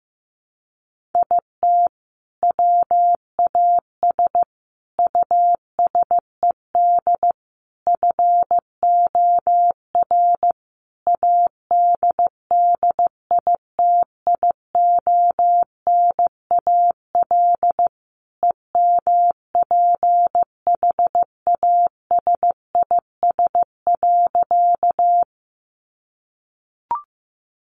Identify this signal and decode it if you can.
Morse code, heard as a keyed tone, about 15 wpm, 700 Hz. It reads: IT WAS USED FOR ADDITIONAL EMPHASIS.
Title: Sentences from Top 1400 Words 15wpm